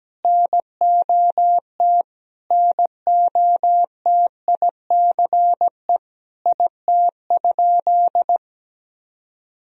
Morse code, heard as a keyed tone, about 17 wpm, 700 Hz. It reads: NOT NOTICE IT?
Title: Sentences from Top 1000 Words - Review Part 2 - 17wpm